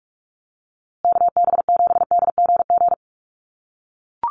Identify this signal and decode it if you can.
Morse code, heard as a keyed tone, about 44 wpm, 700 Hz. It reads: KB7DGG
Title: USA Call Signs Repeated 2x 45wpm